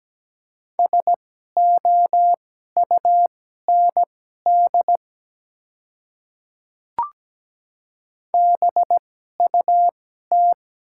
Morse code, heard as a keyed tone, about 17 wpm, spaced as for 12 wpm, 700 Hz. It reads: SOUND BUT
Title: 2x Character Spacing - Top 100 Words - 17wpm